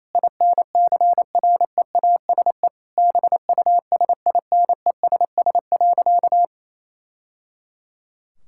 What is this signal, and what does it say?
Morse code, heard as a keyed tone, about 27 wpm, 700 Hz. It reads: INCREASE BUSINESS.
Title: Sentences from Top 1000 Words - Review Part 6 - 28wpm